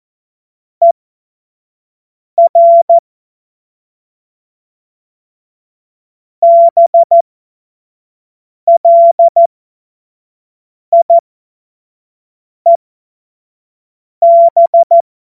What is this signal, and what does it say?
Morse code, heard as a keyed tone, about 14 wpm, spaced as for 5 wpm, 680 Hz. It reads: ER BLIEB
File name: Morsekurs / Witze / Witz 02 Januar 2024 25 F70